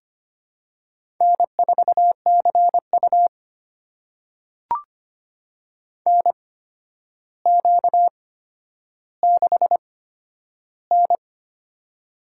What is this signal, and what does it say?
Morse code, heard as a keyed tone, about 25 wpm, spaced as for 11 wpm, 700 Hz. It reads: N4CU NQ6N
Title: USA Call Signs - 4 Characters - 8x Character Spacing - 25wpm